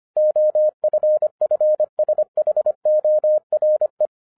Morse code, heard as a keyed tone, about 25 wpm, 610 Hz. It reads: OFFSHORE